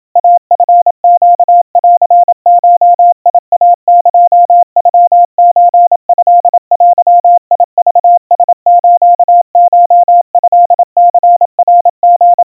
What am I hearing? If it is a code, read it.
Morse code (Japanese Wabun code), heard as a keyed tone, about 27 wpm, 700 Hz. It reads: イチネンゴイエノソトデクラスコトニナリ
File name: kawa / Morse sample / kamyu 100 cpm 700 Hz Crispy